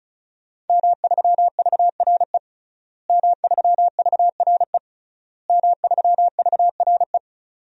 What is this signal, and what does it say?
Morse code, heard as a keyed tone, about 34 wpm, 700 Hz. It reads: M3VRE M3VRE M3VRE
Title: DXCC-100 Call Signs - 5 Characters - Repeated 3x - 35wpm